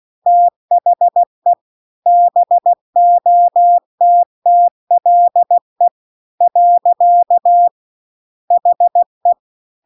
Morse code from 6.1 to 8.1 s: .